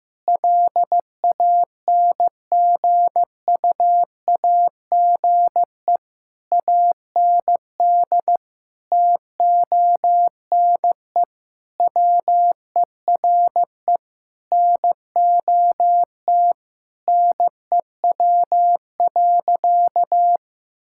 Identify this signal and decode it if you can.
Morse code, heard as a keyed tone, about 15 wpm, 700 Hz. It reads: LANGUAGE AND TONE WERE NOT NEW.
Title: Sentences from Top 1600 Words 15wpm